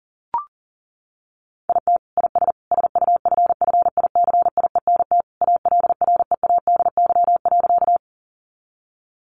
Morse code, heard as a keyed tone, about 40 wpm, 700 Hz. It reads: IT IS SUFFICIENT ALREADY.
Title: Sentences from Top 1400 Words 40wpm